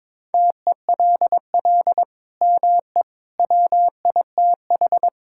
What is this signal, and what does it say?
Morse code, heard as a keyed tone, about 22 wpm, 700 Hz. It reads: TELL ME WITH
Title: Sentences from Top 1000 Words - Review Part 5 - 22wpm